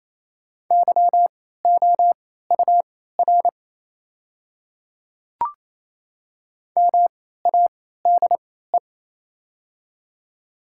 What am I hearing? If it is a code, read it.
Morse code, heard as a keyed tone, about 27 wpm, spaced as for 16 wpm, 700 Hz. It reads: YOUR MADE
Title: 3x Character Spacing - Top 200 Words - 28wpm